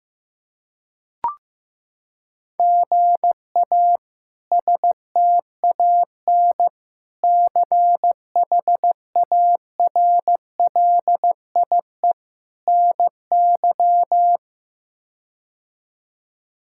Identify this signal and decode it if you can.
Morse code, heard as a keyed tone, about 16 wpm, 700 Hz. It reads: GA STAN CHARLIE NY
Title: K1USN SST QSO Practice - 15wpm